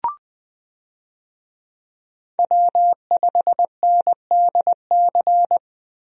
Morse code, heard as a keyed tone, about 20 wpm, 700 Hz. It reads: W5NDC
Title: USA Call Signs - 5 Characters - Repeated 3x - 20wpm